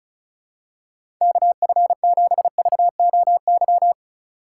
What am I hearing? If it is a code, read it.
Morse code, heard as a keyed tone, about 34 wpm, 700 Hz. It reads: KF7VOY
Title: USA Call Signs - 6 Characters - 2x Character Spacing - 35wpm